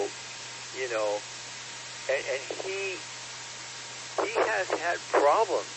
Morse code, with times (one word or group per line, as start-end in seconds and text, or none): none